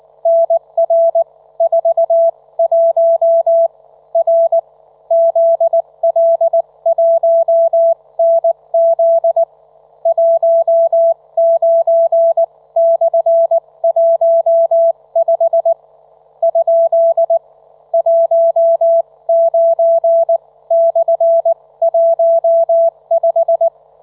Morse code, heard as a keyed tone, about 19 wpm, spaced as for 16 wpm, 670 Hz, with radio noise. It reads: NR41 R ZL1NZ 19/15 ? 19/15
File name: Trivia